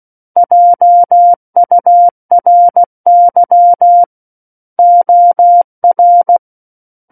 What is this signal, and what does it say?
Morse code, heard as a keyed tone, about 16 wpm, 700 Hz. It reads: JURY OR
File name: 03-Chapter-03 16 WPM